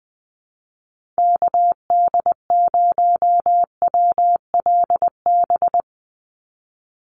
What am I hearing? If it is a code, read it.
Morse code, heard as a keyed tone, about 20 wpm, 700 Hz. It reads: KD0WLB
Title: USA Call Signs 20wpm